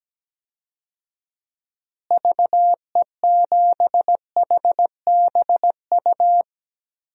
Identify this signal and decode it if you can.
Morse code, heard as a keyed tone, about 17 wpm, 700 Hz. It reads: VE7HBU
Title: DXCC-100 Call Signs - 6 Characters - Repeated 2x - 17wpm